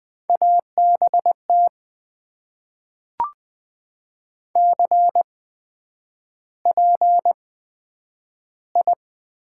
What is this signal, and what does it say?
Morse code, heard as a keyed tone, about 20 wpm, spaced as for 9 wpm, 700 Hz. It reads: ABT CPI
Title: Top 90 QSO Elements - 8x Character Spacing - 20wpm